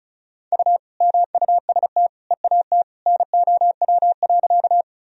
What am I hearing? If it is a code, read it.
Morse code, heard as a keyed tone, about 34 wpm, 700 Hz. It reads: U MUST EAT NOW.
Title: Sentences from Top 300 Words and Abbreviations 35wpm